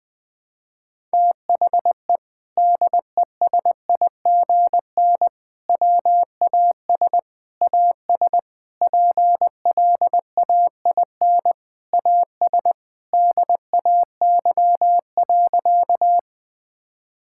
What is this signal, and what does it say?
Morse code, heard as a keyed tone, about 19 wpm, 700 Hz. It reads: THE DESIGN WAS AS PLAIN AS DAY.
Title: Sentences from Top 1000 Words - Review Part 5 - 20wpm